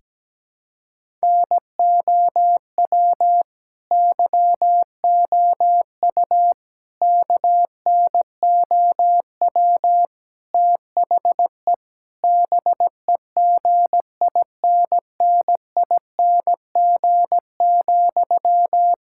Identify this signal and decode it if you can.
Morse code, heard as a keyed tone, about 17 wpm, 700 Hz. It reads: NOW YOU KNOW THE BEGINNING,
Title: Sentences from Top 1000 Words - Review Part 6 - 17wpm